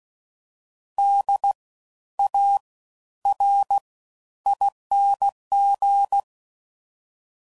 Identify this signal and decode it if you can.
Morse code, heard as a keyed tone, about 16 wpm, spaced as for 11 wpm, 800 Hz. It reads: DARING
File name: Suffixes / Special INGs / Fast Ings